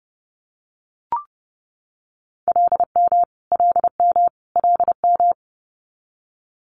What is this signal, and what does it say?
Morse code, heard as a keyed tone, about 30 wpm, 700 Hz. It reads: LM LM LM
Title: Multi-letter Phonemes Repeated 3x 30wpm